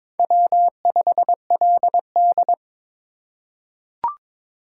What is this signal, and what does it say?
Morse code, heard as a keyed tone, about 21 wpm, 700 Hz. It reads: W5LD
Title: USA Call Signs - 3x Character Spacing - 22wpm